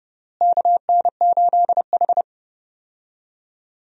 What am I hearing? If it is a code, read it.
Morse code, heard as a keyed tone, about 30 wpm, 700 Hz. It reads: KN8H
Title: USA Call Signs - 4 Characters - Repeated 2x - 30wpm